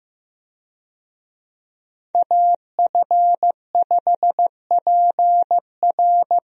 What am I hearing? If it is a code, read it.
Morse code, heard as a keyed tone, about 15 wpm, 700 Hz. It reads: AF5PR
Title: USA Call Signs - 5 Characters - 15wpm